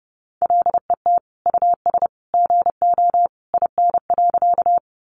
Morse code, heard as a keyed tone, about 28 wpm, 700 Hz. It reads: LET US GO IN.
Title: Sentences from Top 300 Words 30wpm